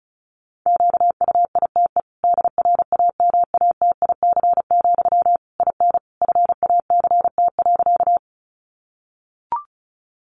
Morse code, heard as a keyed tone, about 34 wpm, 700 Hz. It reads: QUITE DRAMATIC, IN FACT.
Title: Sentences from Top 1300 Words 35wpm